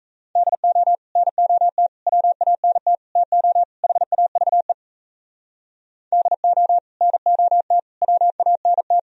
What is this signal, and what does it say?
Morse code, heard as a keyed tone, about 44 wpm, 700 Hz. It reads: DO NOT WANT TO HAVE DO NOT WANT